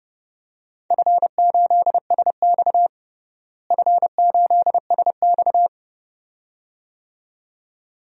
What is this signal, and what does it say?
Morse code, heard as a keyed tone, about 32 wpm, 700 Hz. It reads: F8SX F8SX
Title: DXCC-100 Call Signs - 4 Characters - Repeated 2x - 30wpm